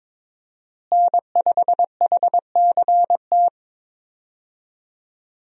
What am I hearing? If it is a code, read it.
Morse code, heard as a keyed tone, about 22 wpm, 700 Hz. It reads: N5HCT
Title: USA Call Signs - 5 Characters - Repeated 3x - 22wpm